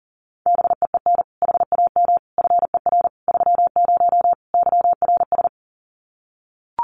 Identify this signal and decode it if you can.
Morse code, heard as a keyed tone, about 40 wpm, 700 Hz. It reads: BEEN HAM FER 30 YRS
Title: Common QSO Phrases Encore 40wpm